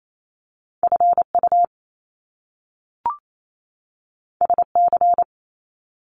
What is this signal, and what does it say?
Morse code, heard as a keyed tone, about 28 wpm, 700 Hz. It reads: FU SC
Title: Top Two Letter Combinations in Words 28wpm